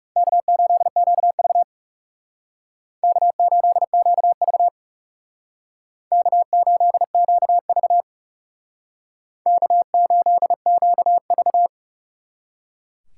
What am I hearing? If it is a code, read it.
Morse code, heard as a keyed tone, about 36 wpm, 700 Hz. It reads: K8QV K8QV K8QV K8QV
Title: Speed Racing - USA 1x2 Call Signs 30wpm